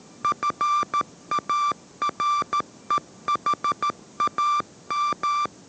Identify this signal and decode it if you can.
Morse code, heard as a keyed tone, about 13 wpm, 1180 Hz. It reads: FAREHAM
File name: 14wpm